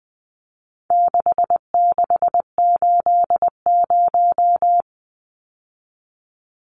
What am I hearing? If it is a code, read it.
Morse code, heard as a keyed tone, about 20 wpm, 700 Hz. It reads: 6680
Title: Numbers - 4-Digit 20wpm